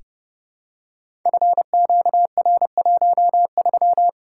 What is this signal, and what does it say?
Morse code, heard as a keyed tone, about 30 wpm, 700 Hz. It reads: FQR13
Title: Code Groups - 5 Letter:Numbers - 30wpm